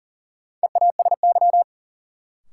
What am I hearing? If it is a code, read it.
Morse code, heard as a keyed tone, about 40 wpm, 700 Hz. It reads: EASY
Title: Sentences from Top 700 Words - New Word Drill - 40wpm